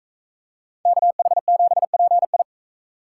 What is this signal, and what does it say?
Morse code, heard as a keyed tone, about 42 wpm, 700 Hz. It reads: KH7PI